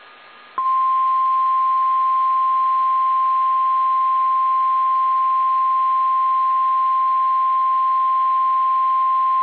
Emergency Alert System (SAME) tones (the 1050 Hz attention tone). It is heard from 1.0 to 9.1 s.